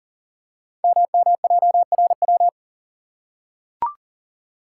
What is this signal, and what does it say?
Morse code, heard as a keyed tone, about 41 wpm, 700 Hz. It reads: MMJRW
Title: Code Groups - 5 Letters - 40wpm